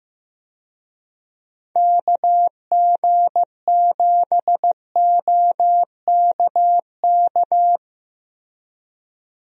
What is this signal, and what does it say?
Morse code, heard as a keyed tone, about 15 wpm, 700 Hz. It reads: KG7OKK
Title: USA Call Signs - 6 Characters - Repeated 2x - 15wpm